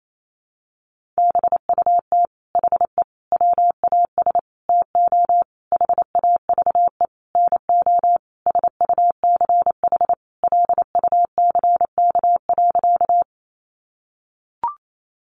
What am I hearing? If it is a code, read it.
Morse code, heard as a keyed tone, about 28 wpm, 700 Hz. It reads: BUT HE WAS TO HAVE NO SUCH LUCK.
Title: Sentences from Top 1500 Words 28wpm